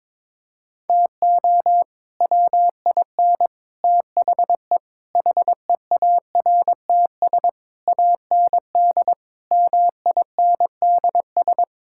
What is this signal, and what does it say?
Morse code, heard as a keyed tone, about 22 wpm, 700 Hz. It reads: TO WIN THE HEARTS AND MINDS
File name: practice / 22wpm / Sets of 8 Words - 22wpm